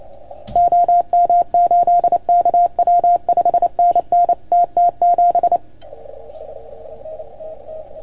Morse code, heard as a keyed tone, about 28 wpm, 690 Hz. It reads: OM8KW 5NN TT7
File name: Archiv / CUC310 / OK1FGD CUC 310 pileup